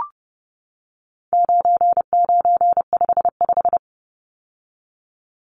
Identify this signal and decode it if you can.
Morse code, heard as a keyed tone, about 30 wpm, 700 Hz. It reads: 9955
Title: Numbers - 4-Digit 30wpm